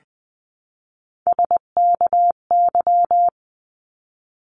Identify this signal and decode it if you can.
Morse code, heard as a keyed tone, about 20 wpm, 700 Hz. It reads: SKY
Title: 259 Lesson 42d - Words 18-20wpm